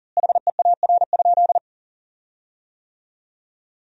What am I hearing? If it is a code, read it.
Morse code, heard as a keyed tone, about 40 wpm, 700 Hz. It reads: HEAR?
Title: 4x Word-Spacing - Sentences from Top 300 Words - 40wpm